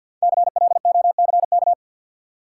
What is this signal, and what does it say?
Morse code, heard as a keyed tone, about 48 wpm, 700 Hz. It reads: CLOCK
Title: Five Letter Words - 50wpm